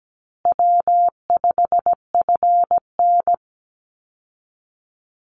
Morse code, heard as a keyed tone, about 17 wpm, 700 Hz. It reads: W5FN
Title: Call Signs Repeated 2x - 17wpm